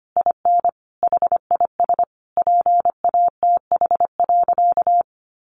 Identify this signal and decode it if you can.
Morse code, heard as a keyed tone, about 25 wpm, 700 Hz. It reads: IN HIS PATH.
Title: Sentences from Top 1400 Words 25wpm